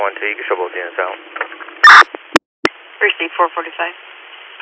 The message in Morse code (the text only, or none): none